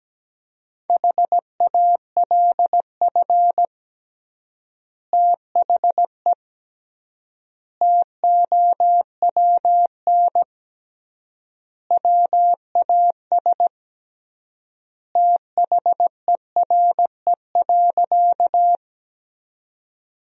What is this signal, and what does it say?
Morse code, heard as a keyed tone, about 17 wpm, 700 Hz. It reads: HALF THE TOWN WAS THERE.
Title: 3x Word-Spacing - Sentences from Top 500 Words - 17wpm